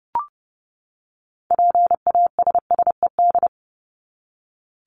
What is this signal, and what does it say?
Morse code, heard as a keyed tone, about 30 wpm, 700 Hz. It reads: PASSED
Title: ED Suffix - Top 100 Words with Base Words 30wpm